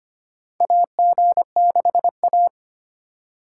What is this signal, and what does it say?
Morse code, heard as a keyed tone, about 25 wpm, 700 Hz. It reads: AG6A
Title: USA Call Signs - 4 Characters - 3x Character Spacing - 25wpm